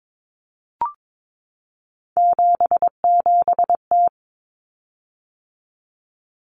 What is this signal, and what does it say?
Morse code, heard as a keyed tone, about 22 wpm, 700 Hz. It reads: 77T
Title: Common Cut Numbers - 3-Digit 22wpm